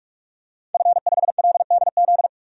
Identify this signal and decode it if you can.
Morse code, heard as a keyed tone, about 46 wpm, 700 Hz. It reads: F5LDZ
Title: DXCC-100 Call Signs - 5 Characters - 45wpm